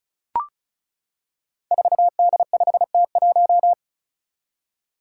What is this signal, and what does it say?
Morse code, heard as a keyed tone, about 36 wpm, 700 Hz. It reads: 4D5T1
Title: Five Challenging Letter-Number Combinations 35wpm